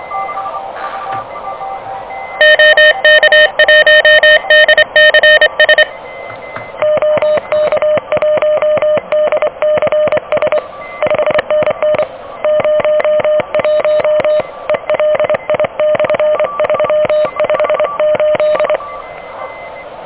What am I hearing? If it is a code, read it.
Morse code, heard as a keyed tone, about 26 wpm, 600 Hz, with noise. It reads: OK1DCS OK1DCS 5NN 01 ELI/358
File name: OK QRP 2023 / OK QRP 2023